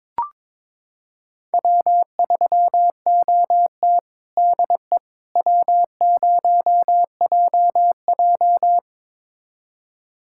Morse code, heard as a keyed tone, about 22 wpm, 700 Hz. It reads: W3OT DE W0JJ
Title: USA Callsign DE Callsign - 4 Characters - 22wpm